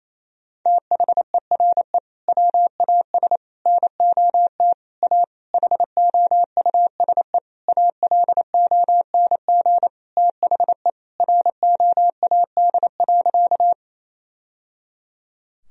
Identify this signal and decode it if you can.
Morse code, heard as a keyed tone, about 28 wpm, 700 Hz. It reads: THERE WAS NOT A HOUSE ALONG THE ROAD.
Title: Sentences from Top 1000 Words - Review Part 2 - 28wpm